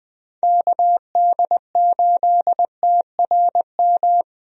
Morse code, heard as a keyed tone, about 20 wpm, 700 Hz. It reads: KD8TRM
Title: USA Call Signs - 6 Characters - Repeated 2x - 20wpm